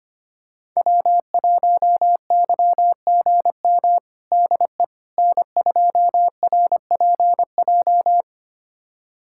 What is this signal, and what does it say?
Morse code, heard as a keyed tone, about 25 wpm, 700 Hz. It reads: W1YGM DE N2RPJ
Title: USA Callsign DE Callsign - 5 Characters - 25wpm